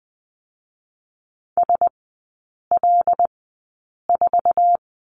Morse code, heard as a keyed tone, about 20 wpm, 700 Hz. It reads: S L 4
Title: 114 Lesson 19e - Letter w - Three Characters 20wpm